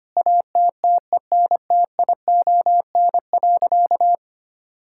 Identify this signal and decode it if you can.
Morse code, heard as a keyed tone, about 25 wpm, 700 Hz. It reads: ATTENTION.
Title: Sentences from Top 1000 Words - Review Part 6 - 25wpm